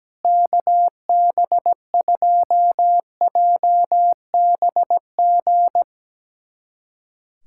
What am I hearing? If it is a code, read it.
Morse code, heard as a keyed tone, about 17 wpm, 700 Hz. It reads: KB2JBG
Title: USA Callsign DE Callsign - 6 Characters - 17wpm